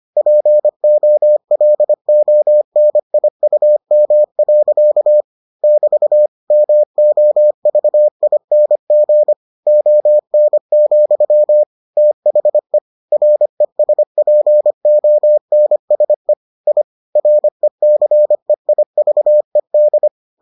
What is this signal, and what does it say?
Morse code, heard as a keyed tone, about 25 wpm, 600 Hz. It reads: POLONIUM. <BT> MOVING ON, THE RESPONSE I RECEIVED